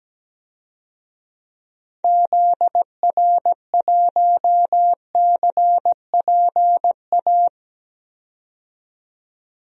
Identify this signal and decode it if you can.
Morse code, heard as a keyed tone, about 17 wpm, 700 Hz. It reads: ZR1CPA
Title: DXCC-100 Call Signs - 6 Characters - 17wpm